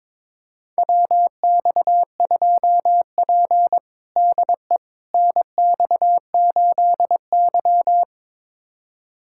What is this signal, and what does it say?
Morse code, heard as a keyed tone, about 22 wpm, 700 Hz. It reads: WX2P DE NX8Y
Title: USA Callsign DE Callsign - 4 Characters - 22wpm